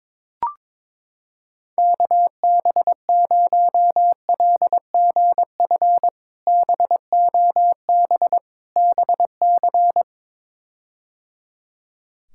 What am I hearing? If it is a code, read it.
Morse code, heard as a keyed tone, about 21 wpm, 700 Hz. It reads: KB0LGF BOB BC
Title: K1USN SST QSO Practice - 22wpm